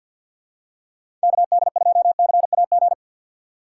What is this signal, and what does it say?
Morse code, heard as a keyed tone, about 49 wpm, 700 Hz. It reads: KD2CAG
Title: USA Call Signs - 6 Characters - 6x Character Spacing - 50wpm